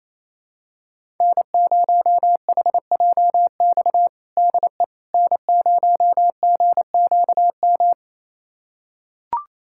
Morse code, heard as a keyed tone, about 28 wpm, 700 Hz. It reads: N0HJX DE N0GQM
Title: USA Callsign DE Callsign - 5 Characters - 28wpm